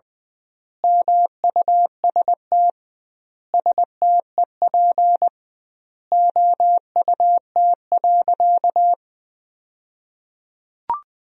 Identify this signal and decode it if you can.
Morse code, heard as a keyed tone, about 20 wpm, 700 Hz. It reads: MUST STEP OUT.
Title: 2x Word-Spacing - Sentences from Top 400 Words - 20wpm